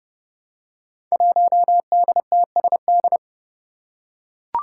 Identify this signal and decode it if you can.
Morse code, heard as a keyed tone, about 30 wpm, 700 Hz. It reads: 1DTSD
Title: Five Challenging Letter-Number Combinations 30wpm